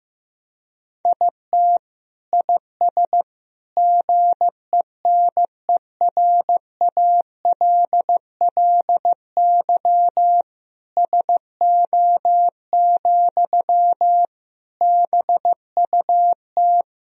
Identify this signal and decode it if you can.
Morse code, heard as a keyed tone, about 15 wpm, 700 Hz. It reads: IT IS GENERALLY SO, BUT
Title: Sentences from Top 1000 Words - Review Part 4 - 15wpm